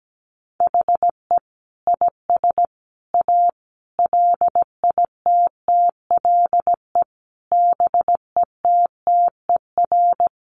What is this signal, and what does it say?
Morse code, heard as a keyed tone, about 17 wpm, 700 Hz. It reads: HE IS A LITTLE BETTER
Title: Sentences from Top 600 Words 17wpm